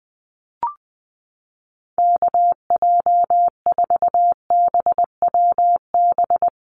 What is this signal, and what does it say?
Morse code, heard as a keyed tone, about 20 wpm, 700 Hz. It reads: KJ4BWB
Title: USA Call Signs 20wpm